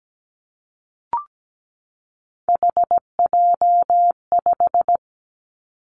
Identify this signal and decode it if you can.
Morse code, heard as a keyed tone, about 17 wpm, 700 Hz. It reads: HJ5
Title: Callsign Prefixes for Top 100 DXCC - 17wpm